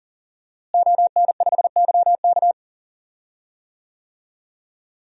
Morse code, heard as a keyed tone, about 40 wpm, 700 Hz. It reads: ON5YK
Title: DXCC-100 Call Signs - 5 Characters - Repeated 2x - 40wpm